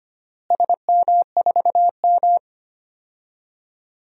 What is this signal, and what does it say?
Morse code, heard as a keyed tone, about 25 wpm, 700 Hz. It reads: SM4M
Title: DXCC-100 Call Signs - 4 Characters - Repeated 3x - 25wpm